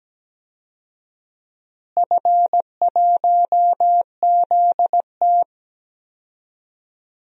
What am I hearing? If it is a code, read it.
Morse code, heard as a keyed tone, about 17 wpm, 700 Hz. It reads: F1ZT
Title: DXCC-100 Call Signs - 4 Characters - Repeated 2x - 17wpm